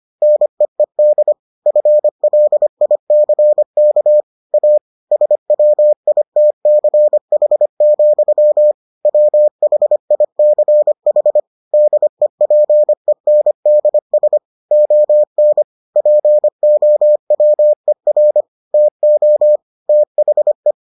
Morse code, heard as a keyed tone, about 25 wpm, 600 Hz. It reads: NEED FLICK A SWITCH, WHICH DEPENDS ON POWER TO THE